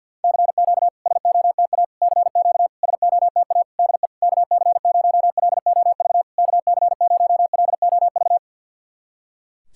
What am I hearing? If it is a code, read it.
Morse code, heard as a keyed tone, about 49 wpm, 690 Hz. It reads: CQ SOTA CQ SOTA DE KC0LOV KC0LOV
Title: SOTA QSO Practice 50wpm